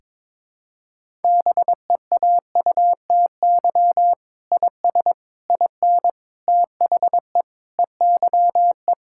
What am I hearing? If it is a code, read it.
Morse code, heard as a keyed tone, about 21 wpm, 700 Hz. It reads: BEAUTY IS IN THE EYE
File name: practice / 22wpm / Sentences with 8-Word Phrases - 22wpm